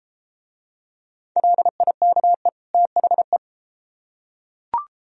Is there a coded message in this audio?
Yes: Morse code from 1.4 to 3.4 s, heard as a keyed tone, about 33 wpm, 700 Hz. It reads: LIKE THE